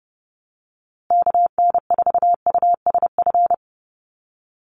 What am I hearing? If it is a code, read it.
Morse code, heard as a keyed tone, about 28 wpm, 700 Hz. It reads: KN4USF
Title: USA Call Signs Repeated 2x 30wpm